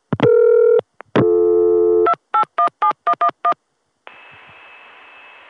DTMF, heard as telephone touch tones, tones of about 80 ms. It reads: 2817111